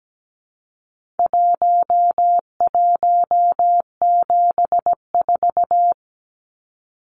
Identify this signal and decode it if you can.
Morse code, heard as a keyed tone, about 17 wpm, 700 Hz. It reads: 1174
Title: Numbers - Contest Style - 4-Digit 17wpm